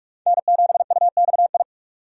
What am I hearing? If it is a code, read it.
Morse code, heard as a keyed tone, about 59 wpm, spaced as for 52 wpm, 700 Hz. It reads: N7UXI
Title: Speed Racing - USA Call Signs 45wpm